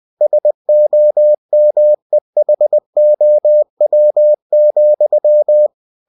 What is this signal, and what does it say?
Morse code, heard as a keyed tone, about 20 wpm, 600 Hz. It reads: SOMEHOW,
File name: cw-books / winnie-the-pooh / winnie-20wpm-page-0042